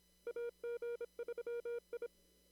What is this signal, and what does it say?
Morse code, heard as a keyed tone, about 27 wpm, 470 Hz. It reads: AG3I
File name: CW / kt5j wpx cw-op-k5pi